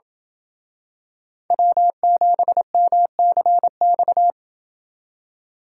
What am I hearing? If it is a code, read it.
Morse code, heard as a keyed tone, about 28 wpm, 700 Hz. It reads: W7MCX